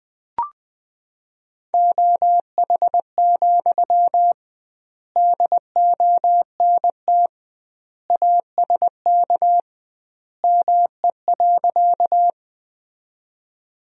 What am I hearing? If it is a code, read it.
Morse code, heard as a keyed tone, about 21 wpm, 700 Hz. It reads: OH, DONT ASK ME.
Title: 2x Word-Spacing - Sentences from Top 500 Words - 20wpm